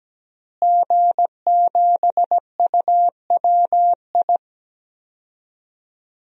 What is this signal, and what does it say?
Morse code, heard as a keyed tone, about 17 wpm, 700 Hz. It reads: G7UWI
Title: DXCC-100 Call Signs - 5 Characters - Repeated 3x - 17wpm